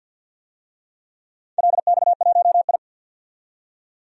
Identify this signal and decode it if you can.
Morse code, heard as a keyed tone, about 47 wpm, 700 Hz. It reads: LX1I
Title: DXCC-100 Call Signs - 4 Characters - 50wpm